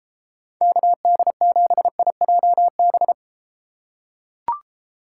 Morse code, heard as a keyed tone, about 33 wpm, 700 Hz. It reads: KD7IJB